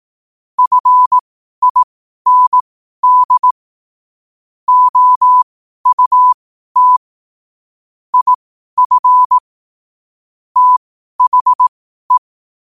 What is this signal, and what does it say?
Morse code, heard as a keyed tone, about 18 wpm, spaced as for 12 wpm, 1000 Hz. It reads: FIND OUT IF THE
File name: qotdfiles / 12wpm